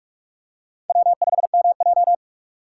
Morse code, heard as a keyed tone, about 42 wpm, 700 Hz. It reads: W5MJ
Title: USA Call Signs - 4 Characters - 4x Character Spacing - 45wpm